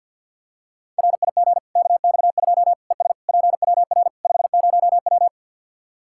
Morse code, heard as a keyed tone, about 51 wpm, 700 Hz. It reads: RIG KX2 ES PWR 50W
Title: Practice QSO - Exchange 2 - Info - 50wpm